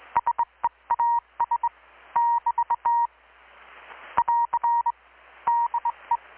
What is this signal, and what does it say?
Morse code, heard as a keyed tone, about 20 wpm, 940 Hz. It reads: SEAS <BT> <AR> DE
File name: blog / swl / 6-abril-2019-12695-USB-KPS